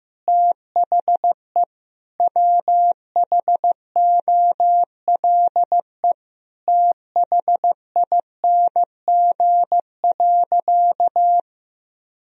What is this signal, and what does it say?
Morse code, heard as a keyed tone, about 15 wpm, 700 Hz. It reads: THE WHOLE THING.
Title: Sentences from Top 1000 Words - Review Part 1 - 15wpm